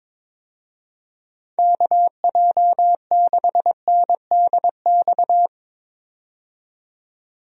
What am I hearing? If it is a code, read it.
Morse code, heard as a keyed tone, about 22 wpm, 700 Hz. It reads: KJ6NDX
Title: USA Call Signs - 6 Characters - 22wpm